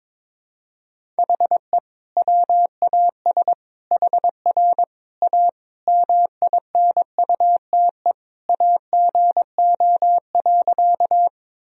Morse code, heard as a keyed tone, about 22 wpm, 700 Hz. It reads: HE WAS HR A MINUTE AGO.
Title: Sentences from Top 500 Words and Abbreviations 22wpm